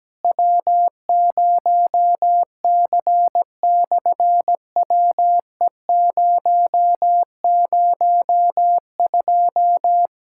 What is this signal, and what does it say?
Morse code, heard as a keyed tone, about 17 wpm, 700 Hz. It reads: W0C/WE002
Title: SOTA QSO Practice 17wpm